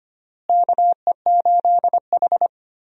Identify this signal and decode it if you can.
Morse code, heard as a keyed tone, about 25 wpm, 700 Hz. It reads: KE8H
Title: USA Call Signs - 4 Characters - Repeated 3x - 25wpm